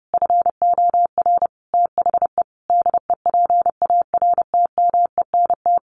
Morse code, heard as a keyed tone, about 30 wpm, 700 Hz. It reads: FOR THE DEPARTMENT
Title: Sentences from Top 1500 Words 30wpm